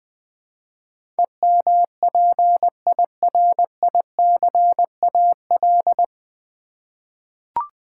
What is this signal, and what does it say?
Morse code, heard as a keyed tone, about 20 wpm, 700 Hz. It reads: EMPIRICAL
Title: Nine Letter Words - 20wpm